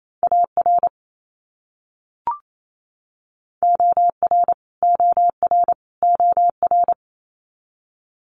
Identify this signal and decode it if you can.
Morse code, heard as a keyed tone, about 27 wpm, 700 Hz. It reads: AR OR OR OR
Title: US States Repeated 3x 28wpm